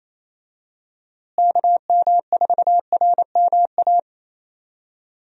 Morse code, heard as a keyed tone, about 28 wpm, 700 Hz. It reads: KM4RMA
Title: USA Call Signs - 6 Characters - Repeated 2x - 28wpm